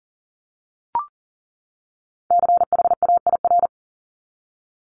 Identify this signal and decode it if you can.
Morse code, heard as a keyed tone, about 40 wpm, 700 Hz. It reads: CHAIR
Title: Top 1751-2000 Words 40wpm